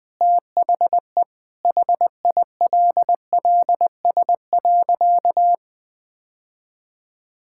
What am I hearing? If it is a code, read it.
Morse code, heard as a keyed tone, about 20 wpm, 700 Hz. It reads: THE HILLS.
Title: Sentences with 2-Word Phrases - Spoken Beginning - 20wpm